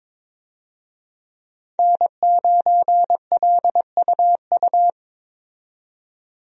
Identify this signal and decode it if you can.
Morse code, heard as a keyed tone, about 22 wpm, 700 Hz. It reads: N9LUU
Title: USA Call Signs - 5 Characters - 22wpm